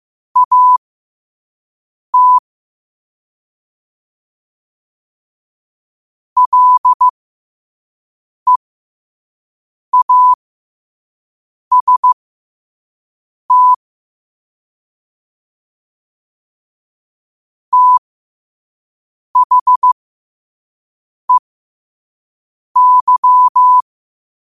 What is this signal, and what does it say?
Morse code, heard as a keyed tone, about 15 wpm, spaced as for 5 wpm, 1000 Hz. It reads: AT LEAST THEY